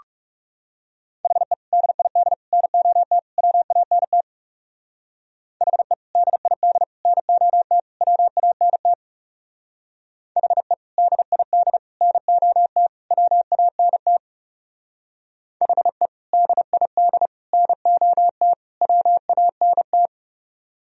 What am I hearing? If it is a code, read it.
Morse code, heard as a keyed tone, about 37 wpm, 700 Hz. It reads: HE DID NOT WANT HE DID NOT WANT HE DID NOT WANT HE DID NOT WANT
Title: Speed-Racing - Sets of 4 Words 30wpm